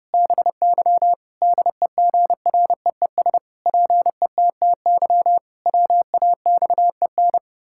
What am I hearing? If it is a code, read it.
Morse code, heard as a keyed tone, about 30 wpm, 700 Hz. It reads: BY DEGREES PETTY WAXEN
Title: Pangram Challenge v2 - 30wpm